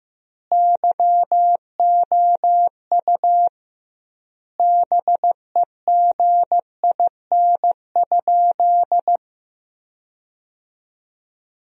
Morse code, heard as a keyed tone, about 15 wpm, 700 Hz. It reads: YOU BEGIN?
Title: 2x Word-Spacing - Sentences from Top 300 Words - 15wpm